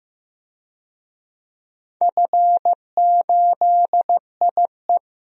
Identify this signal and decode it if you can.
Morse code, heard as a keyed tone, about 15 wpm, 700 Hz. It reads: F8IE
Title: DXCC-100 Call Signs - 4 Characters - Repeated 2x - 15wpm